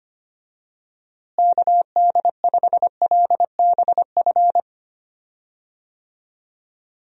Morse code, heard as a keyed tone, about 25 wpm, 700 Hz. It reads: KD5LBF
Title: USA Call Signs - 6 Characters - Repeated 3x - 25wpm